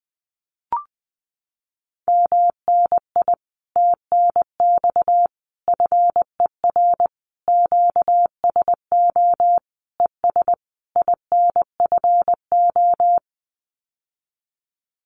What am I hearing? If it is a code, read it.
Morse code, heard as a keyed tone, about 20 wpm, 700 Hz. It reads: MNI TNX FER QSO ES INFO
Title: Common QSO Phrases Encore 20wpm